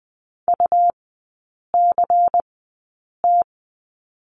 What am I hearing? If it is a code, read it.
Morse code, heard as a keyed tone, about 20 wpm, spaced as for 8 wpm, 700 Hz. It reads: UCT
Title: 166 Lesson 27e - Number 7 - Three Characters 20wpm